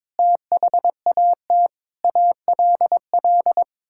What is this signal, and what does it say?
Morse code, heard as a keyed tone, about 22 wpm, 700 Hz. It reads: THAT ALL
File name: Sets of 9 Words - 22wpm